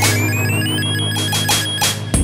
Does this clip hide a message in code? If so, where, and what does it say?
No.